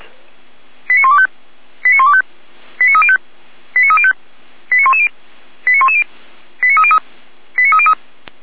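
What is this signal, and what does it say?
Five-tone selective call (ZVEI), heard as quick runs of five tones, tones of about 80 ms. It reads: 89125 89125 89305 89305 89110 89110 89303 89303
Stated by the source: Received by SDR.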